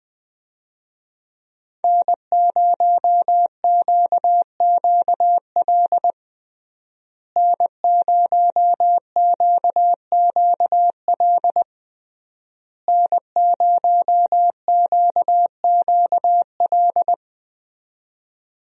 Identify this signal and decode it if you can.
Morse code, heard as a keyed tone, about 20 wpm, 700 Hz. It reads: N0QQL N0QQL N0QQL
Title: USA Call Signs - 5 Characters - Repeated 3x - 20wpm